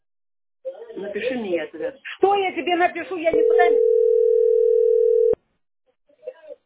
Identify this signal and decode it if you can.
Ringback tone (North America), 440 + 480 Hz, with speech.
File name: db681ef6-7324-43b6-9084